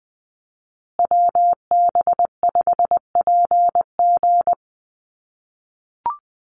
Morse code, heard as a keyed tone, about 20 wpm, 700 Hz. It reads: WB5PG
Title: USA Call Signs Repeated 2x 20wpm